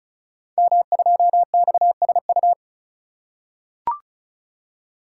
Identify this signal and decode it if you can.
Morse code, heard as a keyed tone, about 36 wpm, 690 Hz. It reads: M2XSU
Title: Code Groups - 5 Letter:Numbers - 35wpm